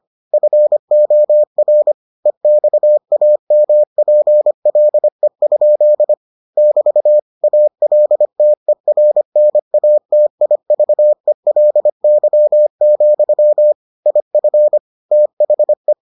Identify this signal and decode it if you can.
Morse code, heard as a keyed tone, about 25 wpm, 600 Hz. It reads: FOR EXAMPLE? <BT> ALTERNATIVELY, IF THE